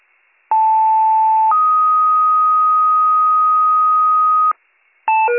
Two-tone page, heard as a one-second tone, then a three-second tone, about 880 then 1250 Hz. It is heard from 0.5 to 4.5 s.